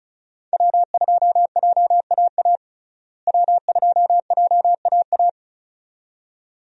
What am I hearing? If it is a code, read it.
Morse code, heard as a keyed tone, about 36 wpm, 700 Hz. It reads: W2JAA W2JAA
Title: USA Call Signs - 5 Characters - Repeated 2x - 35wpm